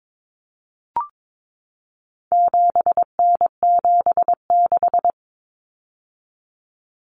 Morse code, heard as a keyed tone, about 22 wpm, 700 Hz. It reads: 7N76
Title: Common Cut Numbers - 4-Digit 22wpm